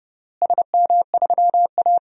Morse code, heard as a keyed tone, about 30 wpm, 700 Hz. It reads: SM3A
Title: DXCC-100 Call Signs - 4 Characters - Repeated 3x - 30wpm